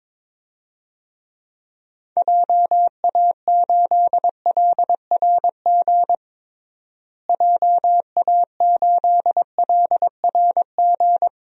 Morse code, heard as a keyed tone, about 22 wpm, 700 Hz. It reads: JA8LRG JA8LRG
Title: DXCC-100 Call Signs - 6 Characters - Repeated 3x - 22wpm